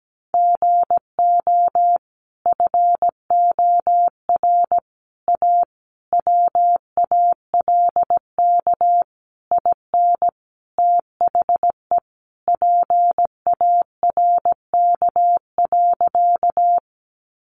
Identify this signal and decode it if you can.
Morse code, heard as a keyed tone, about 17 wpm, 700 Hz. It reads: GO FOR A WALK IN THE PARK.
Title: Sentences from Top 1300 Words 17wpm